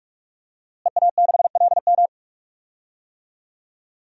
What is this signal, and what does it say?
Morse code, heard as a keyed tone, about 46 wpm, 700 Hz. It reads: EA6LM
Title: DXCC-100 Call Signs - 5 Characters - Repeated 3x - 45wpm